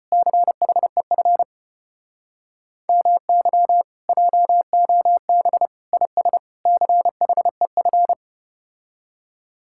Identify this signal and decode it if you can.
Morse code, heard as a keyed tone, about 34 wpm, 700 Hz. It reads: CHEF MY JOB IS CHEF